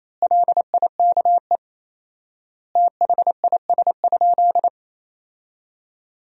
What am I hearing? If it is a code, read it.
Morse code, heard as a keyed tone, about 28 wpm, 700 Hz. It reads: LIKE THIS?
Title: 4x Word-Spacing - Sentences from Top 200 Words - 28wpm